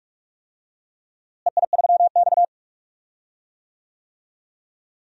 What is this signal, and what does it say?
Morse code, heard as a keyed tone, about 44 wpm, 700 Hz. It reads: EI3X
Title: DXCC-100 Call Signs - 4 Characters - Repeated 3x - 45wpm